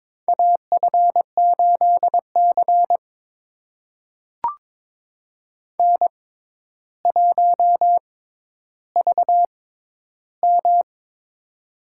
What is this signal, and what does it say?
Morse code, heard as a keyed tone, about 23 wpm, spaced as for 11 wpm, 700 Hz. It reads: AF8C N1VM
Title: USA Call Signs - 4 Characters - 6x Character Spacing - 22wpm